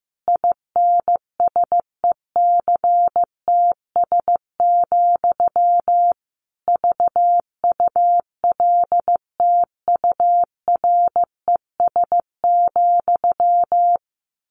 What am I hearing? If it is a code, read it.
Morse code, heard as a keyed tone, about 14 wpm, 700 Hz. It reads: INSECTS, VULTURES,
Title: A Sound Of Thunder 15wpm